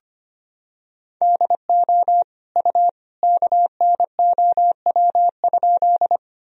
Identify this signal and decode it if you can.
Morse code, heard as a keyed tone, about 25 wpm, 700 Hz. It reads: DO U KNOW?
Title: Sentences from Top 100 Words and Abbreviations 25wpm